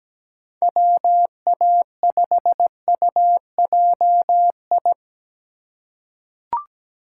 Morse code, heard as a keyed tone, about 17 wpm, 700 Hz. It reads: WA5UJI
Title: USA Call Signs - 6 Characters - 3x Character Spacing - 17wpm